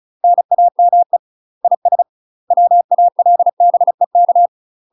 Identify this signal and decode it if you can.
Morse code, heard as a keyed tone, about 34 wpm, 700 Hz. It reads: NAME IS WALBEK
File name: UK MASTER QSOs TRAINING / 35WPM / UK MASTER QSOs 0024